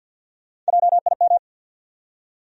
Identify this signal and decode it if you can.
Morse code, heard as a keyed tone, about 49 wpm, 700 Hz. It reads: JIM
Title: USA Common Names - Max 4 Letters - 50wpm